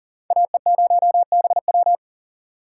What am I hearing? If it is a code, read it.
Morse code, heard as a keyed tone, about 38 wpm, 700 Hz. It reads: AE0BW
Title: USA Call Signs - 3x Character Spacing - 40wpm